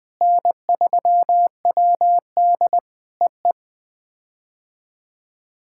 Morse code, heard as a keyed tone, about 20 wpm, 700 Hz. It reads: N3WD EE
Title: POTA QSO Practice - 20wpm